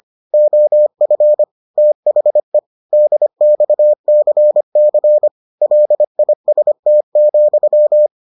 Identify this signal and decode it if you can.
Morse code, heard as a keyed tone, about 25 wpm, 600 Hz. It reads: OF THE DXCC LIST,